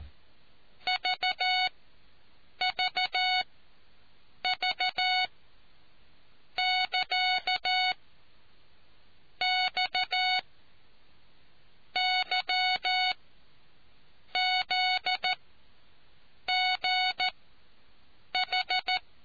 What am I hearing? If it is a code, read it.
Morse code, heard as a keyed tone, about 14 wpm, spaced as for 6 wpm, 740 Hz. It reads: VVV<KA>XYZGH